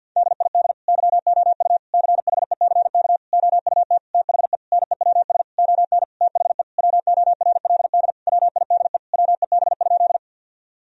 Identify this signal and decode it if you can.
Morse code, heard as a keyed tone, about 50 wpm, 700 Hz. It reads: DID YOU CHECK OUT THE NEWS ON THE WORLD WIDE WEB?
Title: Sentences from Top 1000 Words - Review Part 4 - 50wpm